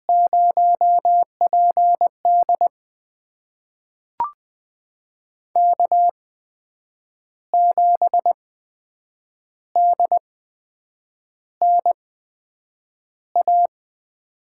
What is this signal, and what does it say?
Morse code, heard as a keyed tone, about 20 wpm, spaced as for 7 wpm, 700 Hz. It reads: 0PD K7DNA
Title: USA Call Signs - 5 Characters - 8x Character Spacing - 20wpm